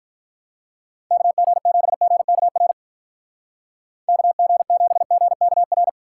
Morse code, heard as a keyed tone, about 46 wpm, 700 Hz. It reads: KG7GKR KG7GKR